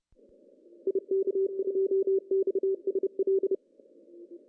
Morse code, heard as a keyed tone, about 30 wpm, 380 Hz. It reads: IK2XSL
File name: CW / cqww-cw-k5tr